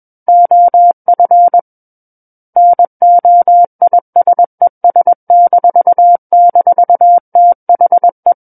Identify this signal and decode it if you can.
Morse code, heard as a keyed tone, about 21 wpm, 700 Hz. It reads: OF NOISES--THE
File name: Book One / 12-Chapter 12 - 21 WPM